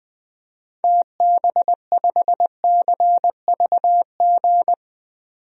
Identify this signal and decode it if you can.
Morse code, heard as a keyed tone, about 20 wpm, 700 Hz. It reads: TB5CVG
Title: DXCC-100 Call Signs - 6 Characters - 20wpm